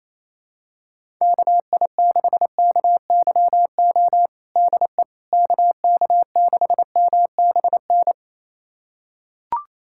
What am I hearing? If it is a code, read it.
Morse code, heard as a keyed tone, about 27 wpm, 700 Hz. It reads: KI6KYO DE KK6MBN
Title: USA Callsign DE Callsign - 6 Characters - 28wpm